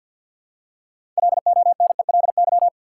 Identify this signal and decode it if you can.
Morse code, heard as a keyed tone, about 48 wpm, 700 Hz. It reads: LONELY
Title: Six Letter Words - 50wpm